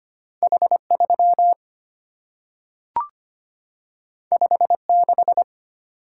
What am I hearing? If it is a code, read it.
Morse code, heard as a keyed tone, about 25 wpm, 700 Hz. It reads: H 3 5 6
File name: Two Letter-Number Combinations - 25wpm